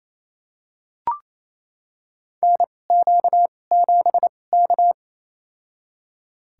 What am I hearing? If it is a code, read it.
Morse code, heard as a keyed tone, about 28 wpm, spaced as for 20 wpm, 700 Hz. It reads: NQ7K
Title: USA Call Signs - 4 Characters - 2x Character Spacing - 28wpm